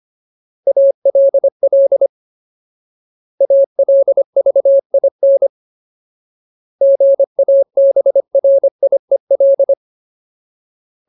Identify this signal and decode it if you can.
Morse code, heard as a keyed tone, about 25 wpm, 550 Hz. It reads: ALL ALVIN GABRIEL
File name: Ultimate Word List / 25WPM Ultimate